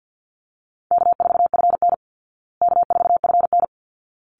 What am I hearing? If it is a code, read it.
Morse code, heard as a keyed tone, about 49 wpm, 700 Hz. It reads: K4FN K4FN
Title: Easy Call Signs Repeated 2x - 50wpm